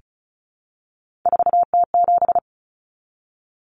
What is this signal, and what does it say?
Morse code, heard as a keyed tone, about 34 wpm, 700 Hz. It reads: 4T7
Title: Common Cut Numbers - 3-Digit 35wpm